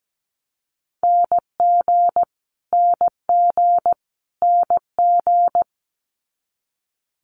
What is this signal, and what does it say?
Morse code, heard as a keyed tone, about 17 wpm, 700 Hz. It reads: NG NG NG
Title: Multi-letter Phonemes Repeated 3x 17wpm